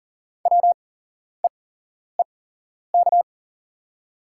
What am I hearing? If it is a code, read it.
Morse code, heard as a keyed tone, about 40 wpm, spaced as for 11 wpm, 700 Hz. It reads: WEEK
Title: 8x Character Spacing - Top 500 Words - 40wpm